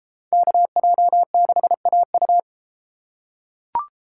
Morse code, heard as a keyed tone, about 33 wpm, 700 Hz. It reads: KJ6AU